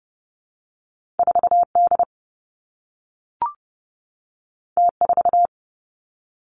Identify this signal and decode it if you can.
Morse code, heard as a keyed tone, about 30 wpm, 700 Hz. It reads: 4 D T 4
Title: Two Challenging Letter-Number Combinations 30wpm